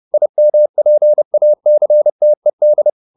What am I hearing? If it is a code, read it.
Morse code, heard as a keyed tone, about 30 wpm, 600 Hz. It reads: IMPACTED
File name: IM8 30